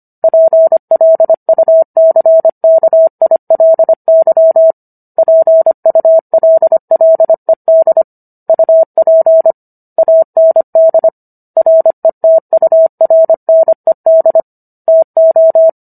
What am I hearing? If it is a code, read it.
Morse code, heard as a keyed tone, about 25 wpm, 650 Hz. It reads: PLUCKILY PULLED UP AND RETURNED TO